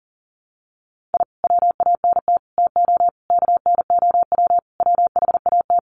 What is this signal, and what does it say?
Morse code, heard as a keyed tone, about 37 wpm, 700 Hz. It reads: I WANT TO KNOW WHAT
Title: Sets of 5 Words Encore2 40wpm